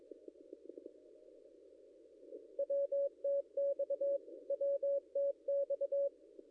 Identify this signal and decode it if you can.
Morse code, heard as a keyed tone, about 23 wpm, 560 Hz. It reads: WTX WTX
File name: contests / ss-cw-k5tr-op-n5rz -0800